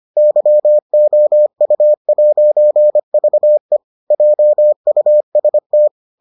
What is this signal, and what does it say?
Morse code, heard as a keyed tone, about 25 wpm, 600 Hz. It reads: YOU'VE JUST